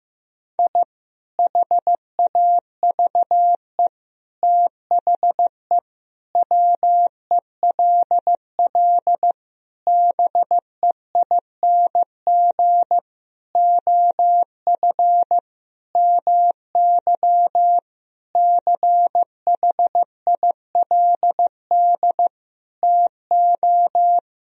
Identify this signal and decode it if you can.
Morse code, heard as a keyed tone, about 15 wpm, 700 Hz. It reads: I HAVE THE WELL BEING OF MY CHILD TO
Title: Sentences from Top 1000 Words - Review Part 3 - 15wpm